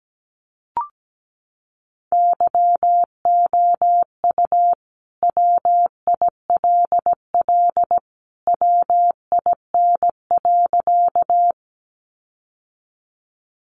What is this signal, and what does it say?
Morse code, heard as a keyed tone, about 17 wpm, 700 Hz. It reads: YOU WILL WIN.
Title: Sentences from Top 1500 Words 17wpm